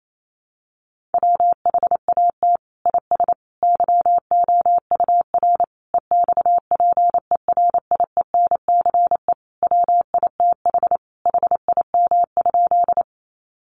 Morse code, heard as a keyed tone, about 28 wpm, 700 Hz. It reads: WHAT IS YOUR EXPERIENCE WITH HIM?
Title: Sentences from Top 600 Words 28wpm